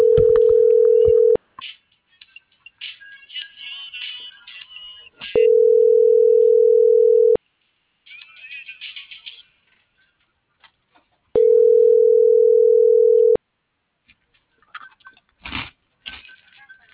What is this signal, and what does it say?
Ringback tone (North America), 440 + 480 Hz.